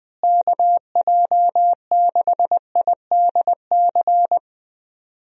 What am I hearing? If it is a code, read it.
Morse code, heard as a keyed tone, about 21 wpm, 700 Hz. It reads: KJ6IDC
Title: USA Call Signs - 6 Characters - Repeated 2x - 20wpm